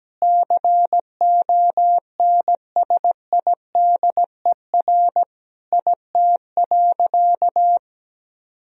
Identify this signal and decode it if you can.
Morse code, heard as a keyed tone, about 17 wpm, 700 Hz. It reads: CONSIDER IT.
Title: Sentences from Top 1000 Words - Review Part 4 - 17wpm